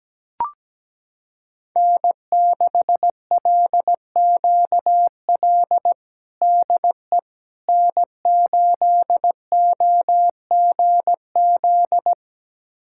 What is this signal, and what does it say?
Morse code, heard as a keyed tone, about 17 wpm, 700 Hz. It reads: N6LQL DE N8OGZ
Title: USA Callsign DE Callsign - 5 Characters - 17wpm